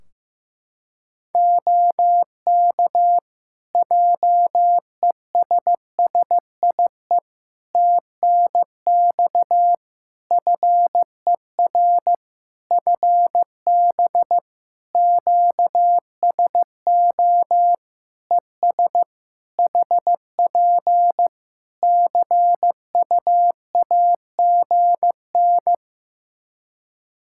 Morse code, heard as a keyed tone, about 15 wpm, 700 Hz. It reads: OK JESSIE TNX FER FB QSO ES HP CUAGN
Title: Practice QSO - Exchange 4 - Ending - 15wpm